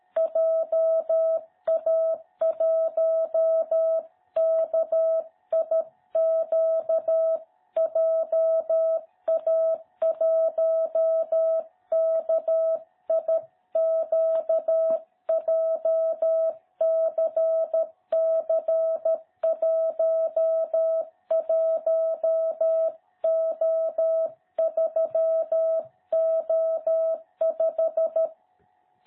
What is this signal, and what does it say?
Morse code, heard as a keyed tone, about 13 wpm, 660 Hz. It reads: JA1KIQ JA1KIQ JCC11O3O5
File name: jh8cbh / 2021cw